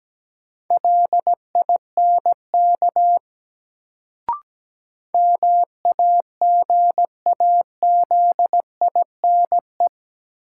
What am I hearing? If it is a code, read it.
Morse code, heard as a keyed tone, about 17 wpm, 700 Hz. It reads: LINK MAGAZINE
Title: Sentences from Top 1000 Words - Review Part 7 - New Word Drill - 17wpm